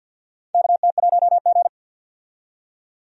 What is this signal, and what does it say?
Morse code, heard as a keyed tone, about 49 wpm, 700 Hz. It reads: KT1G
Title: USA Call Signs - 4 Characters - 50wpm